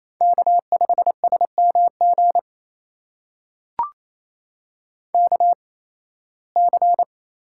Morse code, heard as a keyed tone, about 28 wpm, spaced as for 18 wpm, 700 Hz. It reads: K5SMG KC
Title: USA Call Signs - 5 Characters - 8x Character Spacing - 28wpm